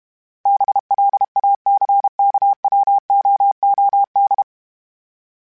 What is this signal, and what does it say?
Morse code, heard as a keyed tone, about 32 wpm, 800 Hz. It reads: BLACKWOOD
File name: fromSMs / FromSMs mp3s / Cities@30